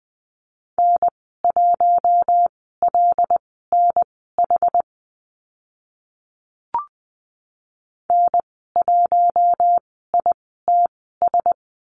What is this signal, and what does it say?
Morse code, heard as a keyed tone, about 20 wpm, spaced as for 14 wpm, 700 Hz. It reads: N1LNH N1ITS
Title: 076 Lesson 13g - Letter L - Call Signs 20wpm